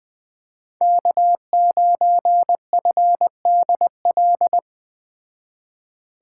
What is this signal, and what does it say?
Morse code, heard as a keyed tone, about 21 wpm, 700 Hz. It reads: K9FDL
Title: Speed Racing - USA 1x3 Call Signs 17wpm